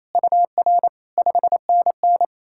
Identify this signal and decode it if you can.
Morse code, heard as a keyed tone, about 38 wpm, spaced as for 33 wpm, 700 Hz. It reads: UR 5NN
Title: Speed-Racing Common QSO Phrases - 28wpm